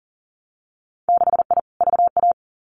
Morse code, heard as a keyed tone, about 42 wpm, 700 Hz. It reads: 6I VA
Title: Winter Field Day Exchanges Encore 40wpm